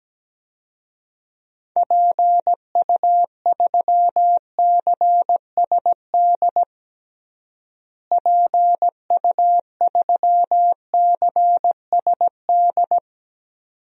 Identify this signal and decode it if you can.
Morse code, heard as a keyed tone, about 17 wpm, 700 Hz. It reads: PU3CSD PU3CSD
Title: DXCC-100 Call Signs - 6 Characters - Repeated 2x - 17wpm